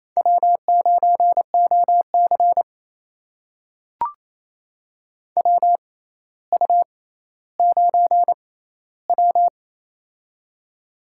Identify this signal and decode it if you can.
Morse code, heard as a keyed tone, about 27 wpm, spaced as for 14 wpm, 700 Hz. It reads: W9OC WU9W
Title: USA Call Signs - 4 Characters - 6x Character Spacing - 28wpm